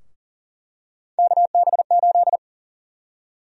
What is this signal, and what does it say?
Morse code, heard as a keyed tone, about 40 wpm, 700 Hz. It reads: KB8
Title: USA Call Sign Prefixes - 40wpm